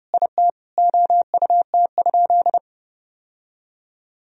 Morse code, heard as a keyed tone, about 30 wpm, 700 Hz. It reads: IT OUT?
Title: Sentences from Top 1000 Words - Review Part 2 - 30wpm